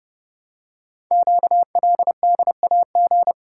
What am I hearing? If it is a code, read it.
Morse code, heard as a keyed tone, about 30 wpm, 700 Hz. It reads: QLDAG
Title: Code Groups - 5 Letters - 30wpm